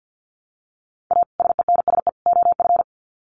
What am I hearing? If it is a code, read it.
Morse code, heard as a keyed tone, about 50 wpm, 700 Hz. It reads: A SENSE OF
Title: Sets of 3 Words 50wpm